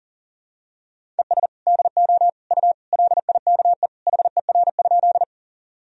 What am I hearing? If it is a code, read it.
Morse code, heard as a keyed tone, about 40 wpm, 700 Hz. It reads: ES DO U LIKE HER?
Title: Sentences from Top 100 Words and Abbreviations 40wpm